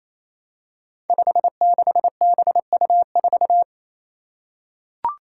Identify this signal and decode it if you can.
Morse code, heard as a keyed tone, about 28 wpm, 700 Hz. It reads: 56BU4
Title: Five Challenging Letter-Number Combinations 28wpm